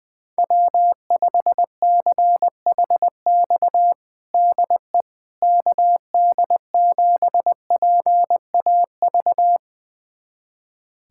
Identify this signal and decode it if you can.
Morse code, heard as a keyed tone, about 20 wpm, 700 Hz. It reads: W5CHX DE KD7PAV
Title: USA Callsign DE Callsign 20wpm